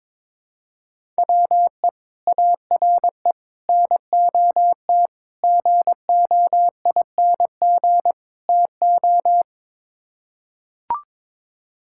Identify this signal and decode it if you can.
Morse code, heard as a keyed tone, about 21 wpm, 700 Hz. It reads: WE ARE NOT GOING TO
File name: practice / 22wpm / Build A Phrase - 5-Word Phrases - 22wpm